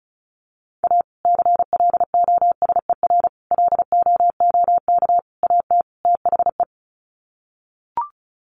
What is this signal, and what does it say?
Morse code, heard as a keyed tone, about 34 wpm, 700 Hz. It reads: A CLOSER LOOK AT THE
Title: Sets of 5 Words Encore 35wpm